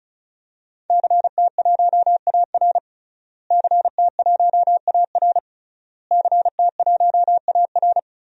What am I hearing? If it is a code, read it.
Morse code, heard as a keyed tone, about 36 wpm, 700 Hz. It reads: CT1AR CT1AR CT1AR
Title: DXCC-100 Call Signs - 5 Characters - Repeated 3x - 35wpm